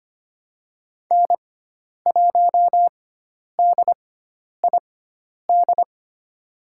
Morse code, heard as a keyed tone, about 25 wpm, spaced as for 10 wpm, 700 Hz. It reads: N1DID
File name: Lesson 16g - Letter c - Call Signs 25wpm